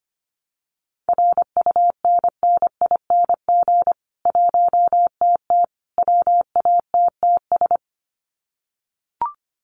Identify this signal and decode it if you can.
Morse code, heard as a keyed tone, about 25 wpm, 700 Hz. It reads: RUNNING 1TT WATTS
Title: Common QSO Phrases 25wpm